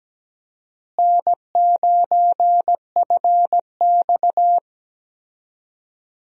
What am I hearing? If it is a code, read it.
Morse code, heard as a keyed tone, about 17 wpm, 700 Hz. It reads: N9FX
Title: USA Call Signs - 4 Characters - Repeated 2x - 17wpm